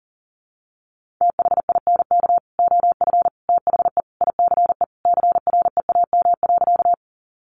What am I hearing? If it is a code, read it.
Morse code, heard as a keyed tone, about 40 wpm, 700 Hz. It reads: THINK OF THE ICE CREAM.
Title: Sentences from Top 1500 Words 40wpm